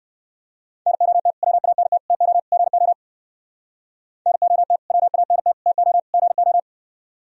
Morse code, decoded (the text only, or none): NOT WANT TO GO NOT WANT TO GO